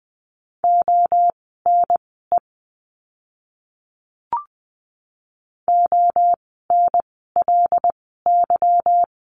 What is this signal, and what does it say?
Morse code, heard as a keyed tone, about 20 wpm, spaced as for 14 wpm, 700 Hz. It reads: ONE ONLY
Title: 160 Lesson 26f - Letter g - Words 20wpm